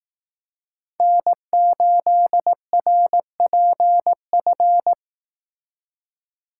Morse code, heard as a keyed tone, about 18 wpm, 700 Hz. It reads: N8RPF
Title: Speed Racing - USA 1x3 Call Signs 15wpm